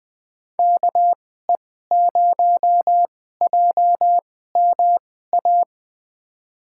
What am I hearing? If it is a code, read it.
Morse code, heard as a keyed tone, about 20 wpm, spaced as for 14 wpm, 700 Hz. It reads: KE0JMA
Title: USA Call Signs - 6 Characters - 2x Character Spacing - 20wpm